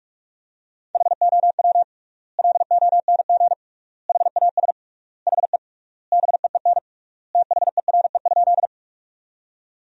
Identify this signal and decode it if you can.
Morse code, heard as a keyed tone, about 41 wpm, 700 Hz. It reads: HOW LONG HAS HE BEEN THERE?
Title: 3x Word-Spacing - Sentences from Top 100 Words - 45wpm